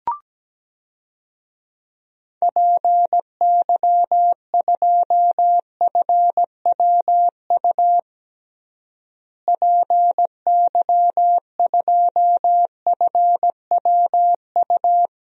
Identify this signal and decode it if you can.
Morse code, heard as a keyed tone, about 17 wpm, 700 Hz. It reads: PY2FWU PY2FWU
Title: DXCC-100 Call Signs - 6 Characters - Repeated 2x - 17wpm